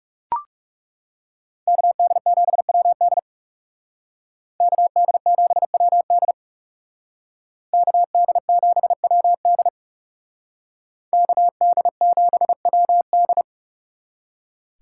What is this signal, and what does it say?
Morse code, heard as a keyed tone, about 34 wpm, 700 Hz. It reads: KD7WD KD7WD KD7WD KD7WD
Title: Speed Racing - USA 2x2 Call Signs 30wpm